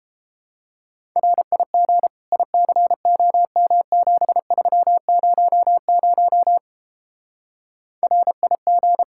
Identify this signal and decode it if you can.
Morse code, heard as a keyed tone, about 29 wpm, 700 Hz. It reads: RIG ICOM7300 RIG